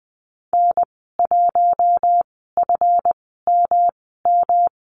Morse code, heard as a keyed tone, about 20 wpm, spaced as for 15 wpm, 700 Hz. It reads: N1FMM
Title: 147 Lesson 24g - Letter y - Call Signs 20wpm